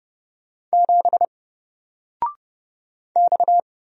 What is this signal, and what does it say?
Morse code, heard as a keyed tone, about 30 wpm, 700 Hz. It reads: 7 X
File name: Single-Letter-Number-30wpm